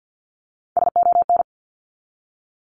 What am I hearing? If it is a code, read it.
Morse code, heard as a keyed tone, about 52 wpm, 700 Hz. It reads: SON
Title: Top 1751-2000 Words 50wpm